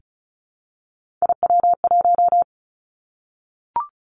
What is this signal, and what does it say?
Morse code, heard as a keyed tone, about 36 wpm, 700 Hz. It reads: IW1
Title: Three Challenging Letter-Number Combinations 35wpm